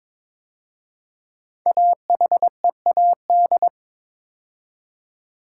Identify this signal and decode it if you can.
Morse code, heard as a keyed tone, about 22 wpm, 690 Hz. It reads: AHEAD
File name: 5 Letter Words - First 3 Letters Given - 22wpm